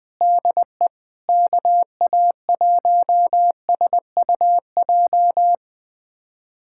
Morse code, heard as a keyed tone, about 20 wpm, 700 Hz. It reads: DE KA1SUJ
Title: USA Callsign DE Callsign 20wpm